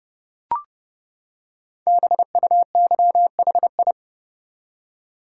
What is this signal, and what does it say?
Morse code, heard as a keyed tone, about 30 wpm, 700 Hz. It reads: BUYHI
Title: Code Groups - 5 Letters - 30wpm